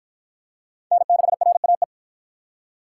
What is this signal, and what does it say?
Morse code, heard as a keyed tone, about 47 wpm, 700 Hz. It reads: N6RAE